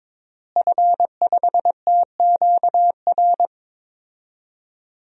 Morse code, heard as a keyed tone, about 22 wpm, 700 Hz. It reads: F5TQR
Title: DXCC-100 Call Signs - 5 Characters - Repeated 3x - 22wpm